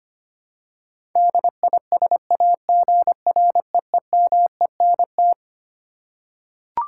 Morse code, heard as a keyed tone, about 25 wpm, 700 Hz. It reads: DISAGREEMENT
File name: Twelve Letter Words - 25wpm